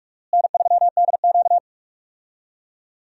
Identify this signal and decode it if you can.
Morse code, heard as a keyed tone, about 46 wpm, 700 Hz. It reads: N3DQ
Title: USA Call Signs - 4 Characters - Repeated 2x - 45wpm